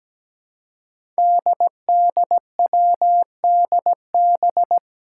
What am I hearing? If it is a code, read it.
Morse code, heard as a keyed tone, about 17 wpm, 700 Hz. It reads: DDWDB
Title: Five Challenging Letter-Number Combinations 17wpm